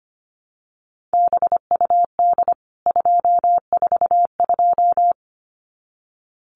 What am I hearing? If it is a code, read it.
Morse code, heard as a keyed tone, about 25 wpm, 700 Hz. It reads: BUD 242
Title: CWT Exchange 25wpm